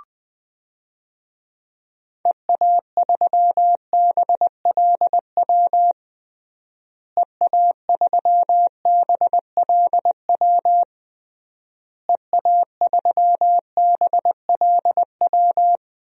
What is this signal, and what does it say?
Morse code, heard as a keyed tone, about 21 wpm, 700 Hz. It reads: EA3BLW EA3BLW EA3BLW
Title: DXCC-100 Call Signs - 6 Characters - Repeated 3x - 20wpm